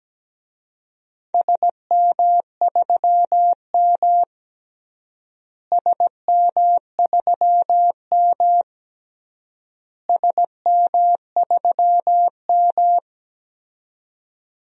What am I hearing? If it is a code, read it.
Morse code, heard as a keyed tone, about 17 wpm, 700 Hz. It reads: SM3M SM3M SM3M
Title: DXCC-100 Call Signs - 4 Characters - Repeated 3x - 17wpm